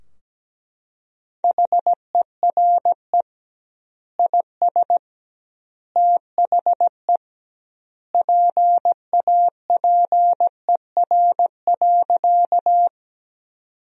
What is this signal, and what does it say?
Morse code, heard as a keyed tone, about 17 wpm, 700 Hz. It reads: HERE IS THE PAPER.
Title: 2x Word-Spacing - Sentences from Top 300 Words - 17wpm